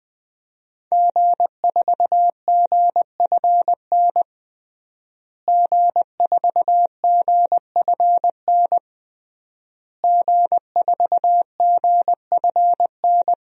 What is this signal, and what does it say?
Morse code, heard as a keyed tone, about 21 wpm, 700 Hz. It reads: G4GFN G4GFN G4GFN
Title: DXCC-100 Call Signs - 5 Characters - Repeated 3x - 20wpm